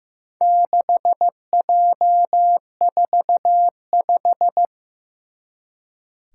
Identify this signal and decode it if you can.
Morse code, heard as a keyed tone, about 15 wpm, 700 Hz. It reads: 6J45
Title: Four Challenging Letter-Number Combinations 15wpm